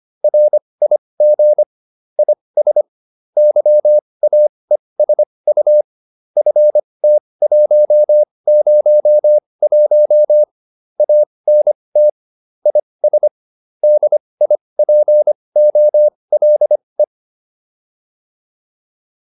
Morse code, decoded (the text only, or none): RIG IS YAESU FT101 ANT IS DIPOLE